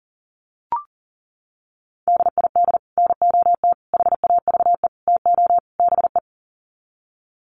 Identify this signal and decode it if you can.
Morse code, heard as a keyed tone, about 40 wpm, 700 Hz. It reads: DID NOT HAVE TO BE
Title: Sets of 5 Words Encore2 40wpm